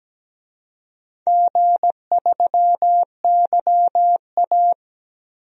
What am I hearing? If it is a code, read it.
Morse code, heard as a keyed tone, about 17 wpm, 700 Hz. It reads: G3YA
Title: DXCC-100 Call Signs - 4 Characters - 17wpm